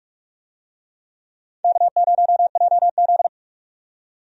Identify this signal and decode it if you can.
Morse code, heard as a keyed tone, about 46 wpm, 700 Hz. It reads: K0JZ
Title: USA Call Signs - 4 Characters - 45wpm